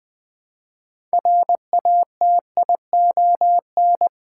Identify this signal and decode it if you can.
Morse code, heard as a keyed tone, about 20 wpm, 690 Hz. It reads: RATION
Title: Build A Word - 2 Letters At A Time - 8-Letter Words - 20wpm